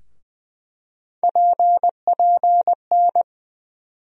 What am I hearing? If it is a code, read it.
Morse code, heard as a keyed tone, about 20 wpm, 700 Hz. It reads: PPN
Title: USA Call Sign Postfixes - 3 Letters 20wpm